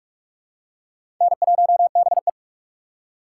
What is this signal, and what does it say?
Morse code, heard as a keyed tone, about 46 wpm, 700 Hz. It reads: N1BE
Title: USA Call Signs - 4 Characters - 4x Character Spacing - 45wpm